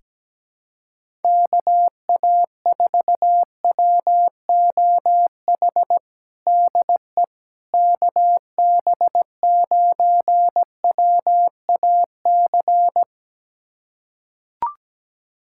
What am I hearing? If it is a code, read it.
Morse code, heard as a keyed tone, about 17 wpm, 700 Hz. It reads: KA4WOH DE KB9WAC
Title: USA Callsign DE Callsign 17wpm